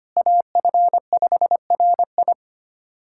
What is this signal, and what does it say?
Morse code, heard as a keyed tone, about 25 wpm, 700 Hz. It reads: AF5RI
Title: USA Callsign DE Callsign - 5 Characters - 25wpm